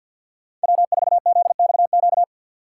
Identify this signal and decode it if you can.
Morse code, heard as a keyed tone, about 52 wpm, 700 Hz. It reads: W4ZXQ
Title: USA Call Signs - 5 Characters - 6x Character Spacing - 50wpm